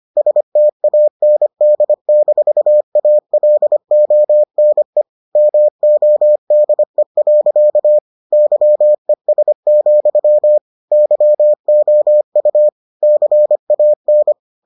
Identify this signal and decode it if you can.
Morse code, heard as a keyed tone, about 25 wpm, 600 Hz. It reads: STAND-ALONE MODE. YES, YOU CAN